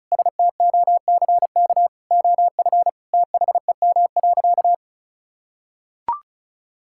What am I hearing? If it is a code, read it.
Morse code, heard as a keyed tone, about 34 wpm, 690 Hz. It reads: STOCK OF THEM.
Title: Sentences from Top 1000 Words - Review Part 5 - 35wpm